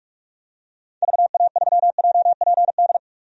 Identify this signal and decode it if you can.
Morse code, heard as a keyed tone, about 46 wpm, 700 Hz. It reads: VA3JPD
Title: DXCC-100 Call Signs - 6 Characters - 45wpm